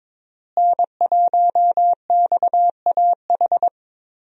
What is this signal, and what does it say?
Morse code, heard as a keyed tone, about 22 wpm, 700 Hz. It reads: N1XAH
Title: USA Call Signs - 5 Characters - Repeated 2x - 22wpm